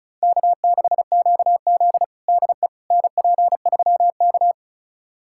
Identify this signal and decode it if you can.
Morse code, heard as a keyed tone, about 36 wpm, 700 Hz. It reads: K6QZ DE NP3K
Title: USA Callsign DE Callsign - 4 Characters - 35wpm